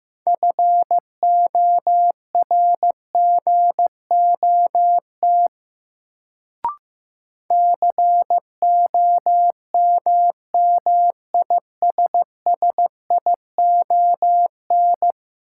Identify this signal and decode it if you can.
Morse code, heard as a keyed tone, about 15 wpm, 700 Hz. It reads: FORGOT COMMISSION
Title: Sentences from Top 1400 Words - New Word Drill Alt - 15wpm